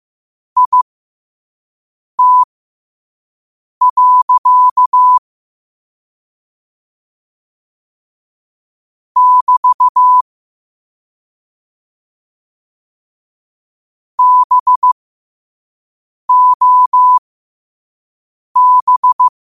Morse code, heard as a keyed tone, about 15 wpm, spaced as for 5 wpm, 1000 Hz. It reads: IT. <BT> BOB
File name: qotdfiles / 5wpm